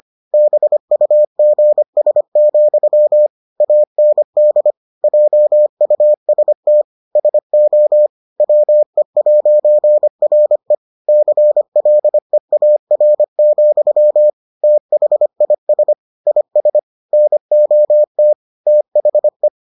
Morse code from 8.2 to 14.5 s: WE'RE CLEAR,

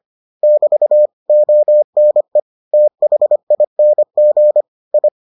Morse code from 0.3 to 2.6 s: <BT> ONE

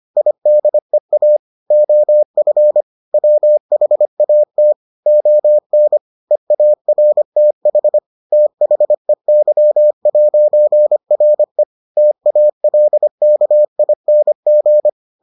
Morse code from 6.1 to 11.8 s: EARTH THEY'RE

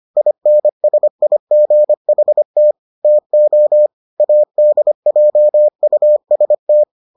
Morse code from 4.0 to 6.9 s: ADJUST

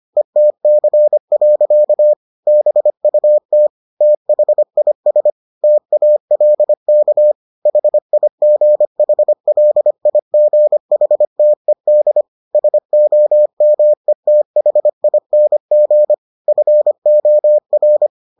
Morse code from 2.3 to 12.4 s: BUT THIS TALK HIGHLIGHTED

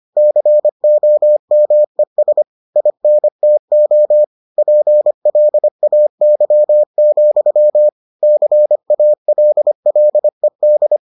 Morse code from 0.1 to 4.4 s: COMES INTO

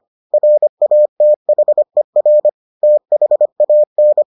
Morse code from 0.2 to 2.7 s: RATHER

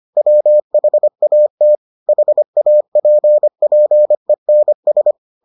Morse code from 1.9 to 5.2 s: HAPPENS

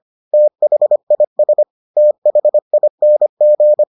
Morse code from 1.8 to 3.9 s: THING

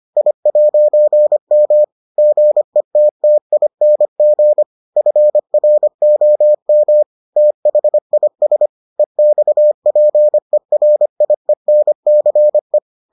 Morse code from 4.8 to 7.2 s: FROM

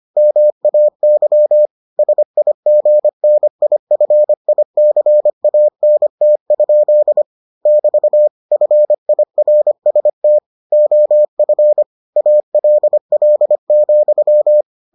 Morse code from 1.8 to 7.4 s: SIGNIFICANT?

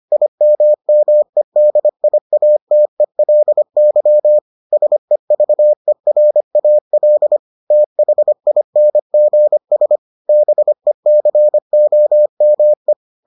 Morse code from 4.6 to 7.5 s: SEVERAL

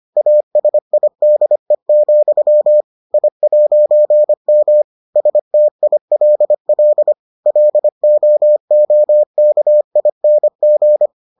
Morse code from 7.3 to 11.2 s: LOOKING

